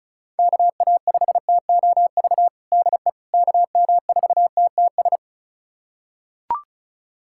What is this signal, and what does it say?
Morse code, heard as a keyed tone, about 34 wpm, 700 Hz. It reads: KA5TOV DE KM4TTS
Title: USA Callsign DE Callsign - 6 Characters - 35wpm